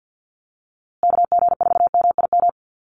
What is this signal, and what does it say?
Morse code, heard as a keyed tone, about 50 wpm, 700 Hz. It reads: KG4MIM
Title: USA Call Signs Repeated 3x 50wpm